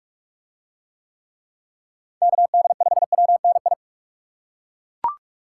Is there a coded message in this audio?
Yes: Morse code from 2.2 to 3.7 s, heard as a keyed tone, about 42 wpm, 700 Hz. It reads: KD5WNI